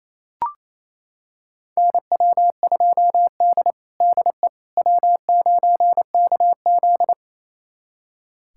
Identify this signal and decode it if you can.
Morse code, heard as a keyed tone, about 27 wpm, 700 Hz. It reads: NW2D DE W9KZ
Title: USA Callsign DE Callsign - 4 Characters - 28wpm